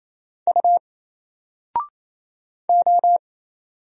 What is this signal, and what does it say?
Morse code, heard as a keyed tone, about 27 wpm, 700 Hz. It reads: U O
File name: Single-Letter-Number-28wpm